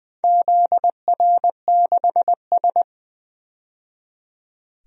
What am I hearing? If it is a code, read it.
Morse code, heard as a keyed tone, about 20 wpm, 700 Hz. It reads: ZR6S
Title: DXCC-100 Call Signs - 4 Characters - 20wpm